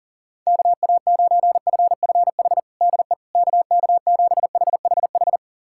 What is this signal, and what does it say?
Morse code, heard as a keyed tone, about 40 wpm, 700 Hz. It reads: KA9FFH DE KK7HHH
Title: USA Callsign DE Callsign - 6 Characters - 40wpm